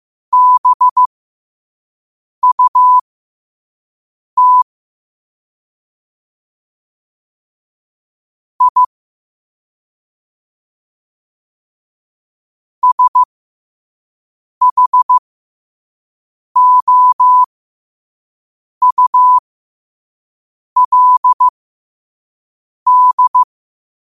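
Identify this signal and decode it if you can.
Morse code, heard as a keyed tone, about 15 wpm, spaced as for 5 wpm, 1000 Hz. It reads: BUT I SHOULD
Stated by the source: Morse sent at 5 wpm.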